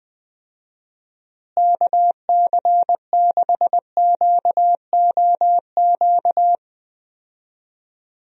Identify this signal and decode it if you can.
Morse code, heard as a keyed tone, about 20 wpm, 700 Hz. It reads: KC6QOQ
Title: USA Call Signs - 6 Characters - Repeated 3x - 20wpm